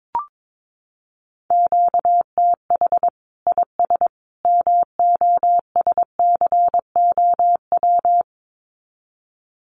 Morse code, heard as a keyed tone, about 22 wpm, 700 Hz. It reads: QTH IS MOSCOW
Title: Common QSO Phrases 22wpm